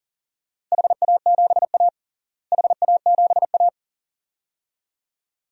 Morse code, heard as a keyed tone, about 40 wpm, 700 Hz. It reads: HA7A HA7A
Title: DXCC-100 Call Signs - 4 Characters - Repeated 2x - 40wpm